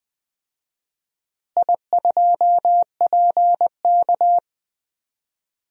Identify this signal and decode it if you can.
Morse code, heard as a keyed tone, about 20 wpm, 700 Hz. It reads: I2PK
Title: DXCC-100 Call Signs - 4 Characters - Repeated 3x - 20wpm